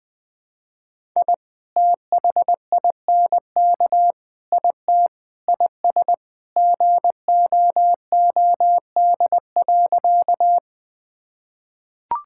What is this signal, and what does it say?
Morse code, heard as a keyed tone, about 20 wpm, 700 Hz. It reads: I THINK IT IS GOOD.
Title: Sentences with 3-Word Phrases - 20wpm